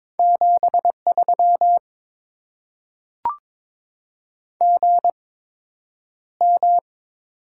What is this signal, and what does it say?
Morse code, heard as a keyed tone, about 22 wpm, spaced as for 9 wpm, 700 Hz. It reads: 73 GM
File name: Top 20 QSO Elements - 8x Character Spacing - 22wpm